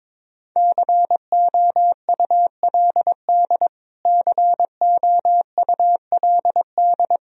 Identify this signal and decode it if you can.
Morse code, heard as a keyed tone, about 22 wpm, 700 Hz. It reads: COULD COULD
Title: Top 100 Words Repeated 22wpm